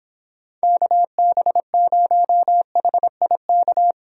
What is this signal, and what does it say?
Morse code, heard as a keyed tone, about 26 wpm, 700 Hz. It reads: KB0HIK